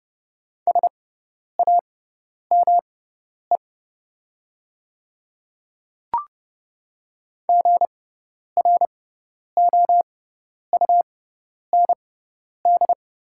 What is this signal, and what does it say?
Morse code, heard as a keyed tone, about 30 wpm, spaced as for 10 wpm, 690 Hz. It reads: SAME GROUND
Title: 6x Character Spacing - Top 500 Words - 30wpm